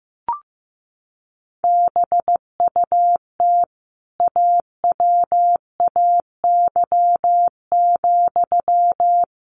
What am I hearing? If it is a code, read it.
Morse code, heard as a keyed tone, about 15 wpm, 700 Hz. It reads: BUT AWAY,
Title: Sentences from Top 1400 Words 15wpm